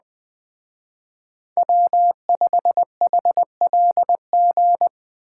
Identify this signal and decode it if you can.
Morse code, heard as a keyed tone, about 20 wpm, 700 Hz. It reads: W5HLG